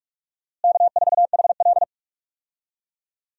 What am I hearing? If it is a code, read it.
Morse code, heard as a keyed tone, about 46 wpm, 700 Hz. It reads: K4HL
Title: USA Call Signs - 4 Characters - Repeated 3x - 45wpm